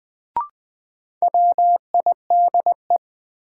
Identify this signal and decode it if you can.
Morse code, heard as a keyed tone, about 20 wpm, 700 Hz. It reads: WIDE
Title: Sets of 4 Words - New Word Drill - 20wpm